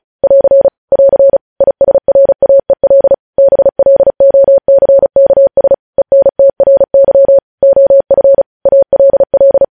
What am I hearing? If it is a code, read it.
Morse code, heard as a keyed tone, about 36 wpm, 550 Hz. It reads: <AR> <AR> ISRAEL BLOCKS ENTRY OF ALL
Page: News Headlines in Morse Code at 35 WPM